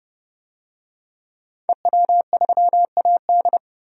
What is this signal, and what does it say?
Morse code, heard as a keyed tone, about 30 wpm, 700 Hz. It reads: EW3AD
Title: DXCC-100 Call Signs - 5 Characters - 30wpm